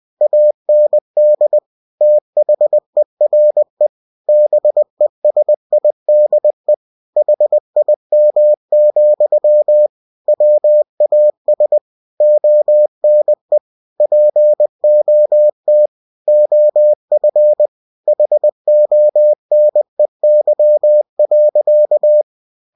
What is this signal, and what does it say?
Morse code, heard as a keyed tone, about 20 wpm, 600 Hz. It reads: AND THERE BESIDE HIM, WAS ONE POT OF HONEY.
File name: cw-books / winnie-the-pooh / winnie-20wpm-page-0094